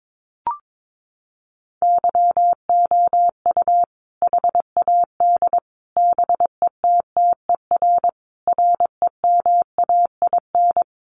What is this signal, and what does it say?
Morse code, heard as a keyed tone, about 22 wpm, 700 Hz. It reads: YOU HAD BETTER REMAIN
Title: Sentences from Top 1200 Words 22wpm